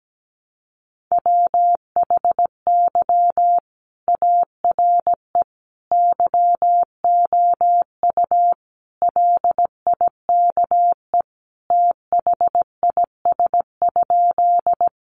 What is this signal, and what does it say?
Morse code, heard as a keyed tone, about 17 wpm, 700 Hz. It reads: WHY ARE YOU LIKE THIS?
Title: Sentences from Top 200 Words 17wpm